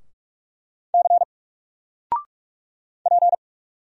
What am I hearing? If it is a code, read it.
Morse code, heard as a keyed tone, about 44 wpm, 700 Hz. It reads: C P
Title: Single Character (Letter:Number:Punctuation:Prosign) 45wpm